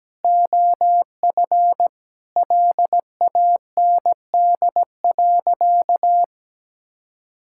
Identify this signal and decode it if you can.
Morse code, heard as a keyed tone, about 17 wpm, 700 Hz. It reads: OF LAND.
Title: Sentences from Top 1000 Words - Review Part 6 - 17wpm